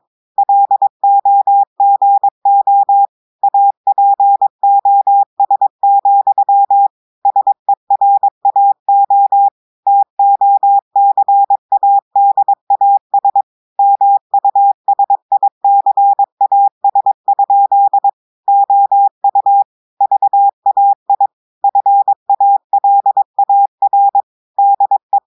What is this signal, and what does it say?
Morse code, heard as a keyed tone, about 22 wpm, 800 Hz. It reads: LOGO APOS, SERAO TOCADAS MUSICAS? OU VAI FALAR DE
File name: audios / portugais / portugais texte en clair 2 vitesse 22 mots minutes